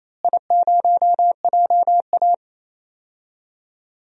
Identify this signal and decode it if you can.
Morse code, heard as a keyed tone, about 28 wpm, 700 Hz. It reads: I0JA
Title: DXCC-100 Call Signs - 4 Characters - Repeated 2x - 28wpm